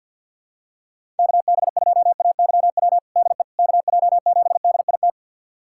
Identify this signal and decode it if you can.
Morse code, heard as a keyed tone, about 50 wpm, 700 Hz. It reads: KB2AYW DE KJ7DIT
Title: USA Callsign DE Callsign - 6 Characters - 50wpm